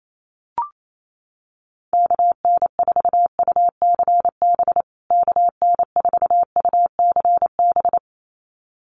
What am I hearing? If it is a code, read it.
Morse code, heard as a keyed tone, about 28 wpm, 700 Hz. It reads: KN4UCB KN4UCB
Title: USA Call Signs Repeated 2x 28wpm